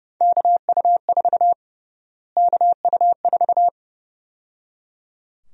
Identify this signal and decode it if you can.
Morse code, heard as a keyed tone, about 30 wpm, 700 Hz. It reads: KU4 KU4
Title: USA Call Sign Prefixes - Repeated 2x - 30wpm